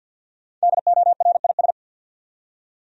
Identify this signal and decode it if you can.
Morse code, heard as a keyed tone, about 47 wpm, 700 Hz. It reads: DORIS
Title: USA Common Names - Max 5 Letters - 50wpm